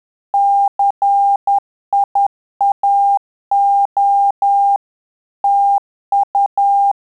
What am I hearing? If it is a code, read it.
Morse code, heard as a keyed tone, about 10 wpm, 800 Hz. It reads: CIAO TU
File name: Telegraphy-group / QSO CW 10 PPM